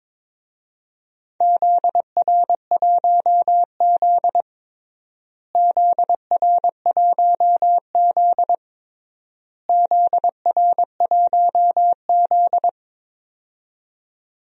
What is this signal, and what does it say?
Morse code, heard as a keyed tone, about 21 wpm, 700 Hz. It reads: ZR1Z ZR1Z ZR1Z
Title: DXCC-100 Call Signs - 4 Characters - Repeated 3x - 22wpm